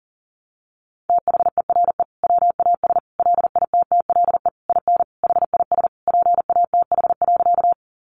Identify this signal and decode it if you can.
Morse code, heard as a keyed tone, about 40 wpm, 700 Hz. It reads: THERE WAS LITTLE IN HIS PATH.
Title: Sentences from Top 1400 Words 40wpm